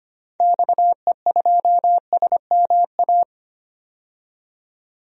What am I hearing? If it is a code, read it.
Morse code, heard as a keyed tone, about 25 wpm, 700 Hz. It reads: XE2SMA
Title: DXCC-100 Call Signs - 6 Characters - 25wpm